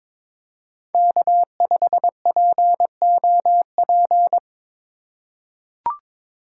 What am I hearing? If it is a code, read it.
Morse code, heard as a keyed tone, about 22 wpm, 700 Hz. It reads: K5POP
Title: USA Call Signs - 5 Characters - 2x Character Spacing - 22wpm